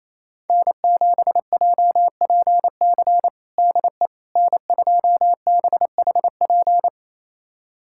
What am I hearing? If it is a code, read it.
Morse code, heard as a keyed tone, about 27 wpm, 690 Hz. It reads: N7JPC DE N2BHP
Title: USA Callsign DE Callsign - 5 Characters - 28wpm